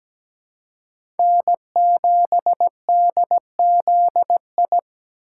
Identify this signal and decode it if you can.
Morse code, heard as a keyed tone, about 17 wpm, 700 Hz. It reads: N7DZI
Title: USA Call Signs - 5 Characters - 4x Character Spacing - 17wpm